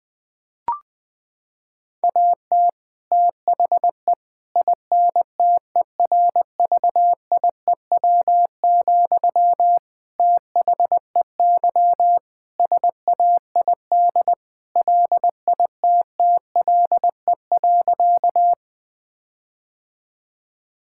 Morse code, heard as a keyed tone, about 19 wpm, 700 Hz. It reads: AT THE INTERVIEW, THEY SAID LITTLE.
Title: Sentences from Top 1600 Words 20wpm